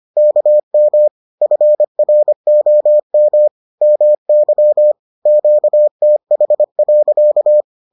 Morse code, heard as a keyed tone, about 25 wpm, 600 Hz. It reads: KM FROM MY QTH.